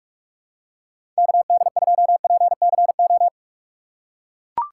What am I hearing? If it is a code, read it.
Morse code, heard as a keyed tone, about 41 wpm, 700 Hz. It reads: KD2PCO
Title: USA Call Signs - 3x Character Spacing - 45wpm